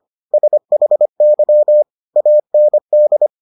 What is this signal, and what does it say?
Morse code, heard as a keyed tone, about 25 wpm, 600 Hz. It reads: SHY AND